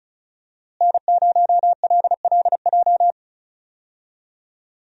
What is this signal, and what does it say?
Morse code, heard as a keyed tone, about 36 wpm, 700 Hz. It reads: N0LLJ
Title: USA Call Signs - 5 Characters - Repeated 2x - 35wpm